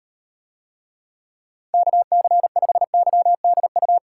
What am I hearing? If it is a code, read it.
Morse code, heard as a keyed tone, about 38 wpm, 700 Hz. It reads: KC5YDU